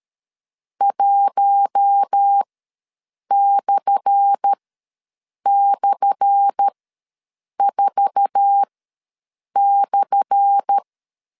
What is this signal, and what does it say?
Morse code, heard as a keyed tone, about 13 wpm, spaced as for 7 wpm, 800 Hz. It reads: 1//4/
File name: LEZIONI / 10 SIGNS / 10A SLASH ws 12wpm 3443